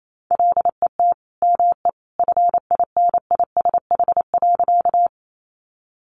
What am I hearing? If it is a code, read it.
Morse code, heard as a keyed tone, about 28 wpm, 700 Hz. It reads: LET ME FINISH.
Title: Sentences from Top 1200 Words 28wpm